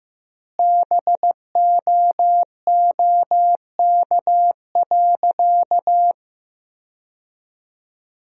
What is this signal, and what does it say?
Morse code, heard as a keyed tone, about 15 wpm, 700 Hz. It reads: BOOK.
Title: 4x Word-Spacing - Sentences from Top 300 Words - 15wpm